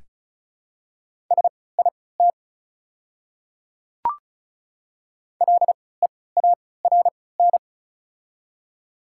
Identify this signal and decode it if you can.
Morse code, heard as a keyed tone, about 36 wpm, spaced as for 20 wpm, 700 Hz. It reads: SIT LEARN
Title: 3x Character Spacing - Top 500 Words - 35wpm